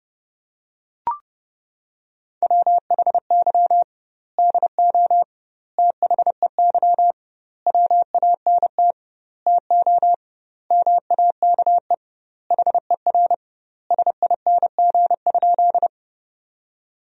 Morse code, heard as a keyed tone, about 32 wpm, 700 Hz. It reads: WHY DO THEY WANT TO MAKE HER SING?
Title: 2x Word-Spacing - Sentences from Top 400 Words - 30wpm